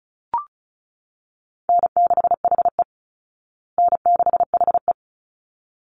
Morse code, heard as a keyed tone, about 34 wpm, 700 Hz. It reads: N6HE N6HE
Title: Call Signs Repeated 3x - 35wpm